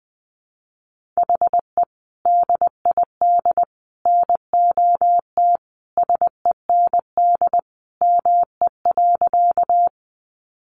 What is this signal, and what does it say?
Morse code, heard as a keyed tone, about 20 wpm, 700 Hz. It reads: HE DID NOT SEND ME.
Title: Sentences from Top 1500 Words 20wpm